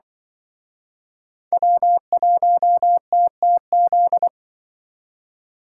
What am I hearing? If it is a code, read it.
Morse code, heard as a keyed tone, about 25 wpm, 700 Hz. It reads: W1TTZ